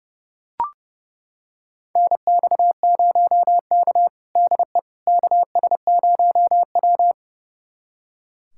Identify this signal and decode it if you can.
Morse code, heard as a keyed tone, about 30 wpm, 700 Hz. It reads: NX0K DE KS0W
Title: USA Callsign DE Callsign - 4 Characters - 30wpm